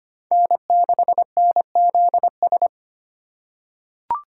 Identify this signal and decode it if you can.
Morse code, heard as a keyed tone, about 25 wpm, 700 Hz. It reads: N6NZS
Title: USA Call Signs - 6x Character Spacing - 25wpm